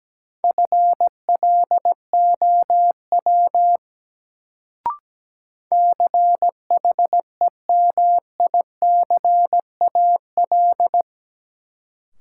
Morse code, heard as a keyed tone, about 17 wpm, 700 Hz. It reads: FLOW CHEMICAL
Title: Sentences from Top 1200 Words - New Word Drill - 17wpm